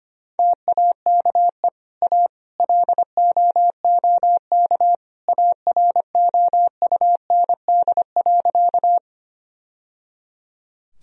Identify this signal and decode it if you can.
Morse code, heard as a keyed tone, about 25 wpm, 700 Hz. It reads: TAKE A LOOK AROUND.
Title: Sentences from Top 1000 Words - Review Part 3 - 25wpm